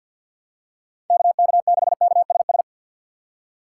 Morse code, heard as a keyed tone, about 49 wpm, 700 Hz. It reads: KK6KSS
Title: USA Call Signs - 6 Characters - 3x Character Spacing - 50wpm